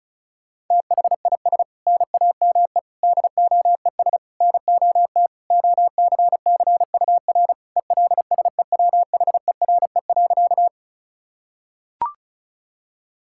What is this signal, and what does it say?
Morse code, heard as a keyed tone, about 36 wpm, 700 Hz. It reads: THIS NAME DOES NOT OCCUR ELSEWHERE.
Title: Sentences from Top 1600 Words 35wpm